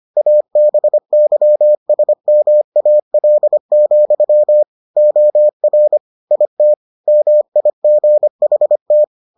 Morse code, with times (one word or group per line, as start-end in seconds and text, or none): 0.2-4.6 ABYSMAL,
5.0-6.0 OR
6.3-6.7 IT
7.1-9.0 MIGHT